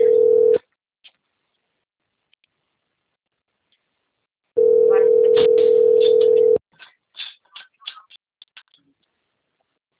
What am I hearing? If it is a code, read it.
Ringback tone (North America), 440 + 480 Hz.